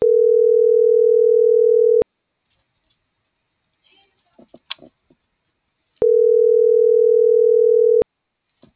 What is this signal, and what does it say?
Ringback tone (North America), 440 + 480 Hz.